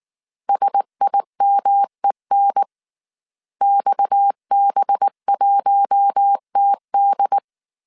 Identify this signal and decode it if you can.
Morse code, heard as a keyed tone, about 19 wpm, 800 Hz. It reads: SIMEN XB1TD